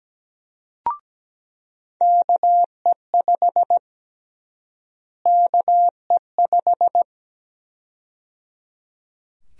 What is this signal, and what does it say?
Morse code, heard as a keyed tone, about 17 wpm, 700 Hz. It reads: KE5 KE5
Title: USA Call Sign Prefixes - Repeated 2x - 17wpm